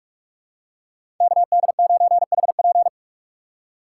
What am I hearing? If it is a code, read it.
Morse code, heard as a keyed tone, about 44 wpm, 690 Hz. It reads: KD9HP
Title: USA Call Signs - 5 Characters - 6x Character Spacing - 45wpm